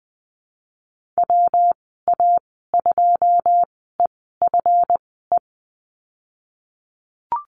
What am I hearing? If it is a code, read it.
Morse code, heard as a keyed tone, about 20 wpm, spaced as for 15 wpm, 700 Hz. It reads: WA2EFE
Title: 140 Lesson 23g - Letter f - Call Signs 20wpm